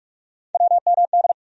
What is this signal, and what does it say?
Morse code, heard as a keyed tone, about 44 wpm, 700 Hz. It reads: WMD
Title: USA Call Sign Postfixes - 3 Letters 45wpm